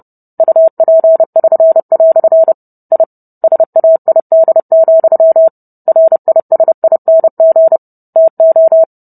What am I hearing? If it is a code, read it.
Morse code, heard as a keyed tone, about 30 wpm, 650 Hz. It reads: UP<SN>" I SAID, RISING TO